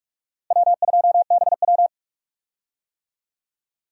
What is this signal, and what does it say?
Morse code, heard as a keyed tone, about 45 wpm, 700 Hz. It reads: W2BW
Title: USA Call Signs - 4 Characters - Repeated 3x - 45wpm